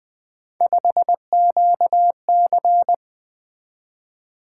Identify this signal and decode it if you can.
Morse code, heard as a keyed tone, about 20 wpm, 700 Hz. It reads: 5QC
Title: Three Letter-Number Combinations - 20wpm